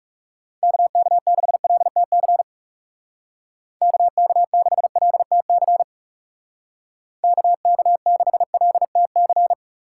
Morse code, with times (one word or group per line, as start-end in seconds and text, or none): 0.6-2.4 KK6LTC
3.8-5.8 KK6LTC
7.2-9.5 KK6LTC